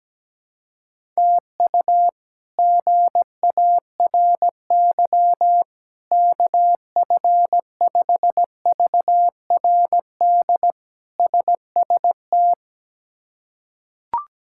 Morse code, heard as a keyed tone, about 17 wpm, 700 Hz. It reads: TU GARY KF5VRD SST
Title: K1USN SST QSO Practice - 17wpm